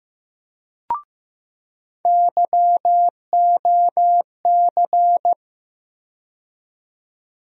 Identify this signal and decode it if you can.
Morse code, heard as a keyed tone, about 15 wpm, 700 Hz. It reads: YOC
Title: USA Call Sign Postfixes - 3 Letters 15wpm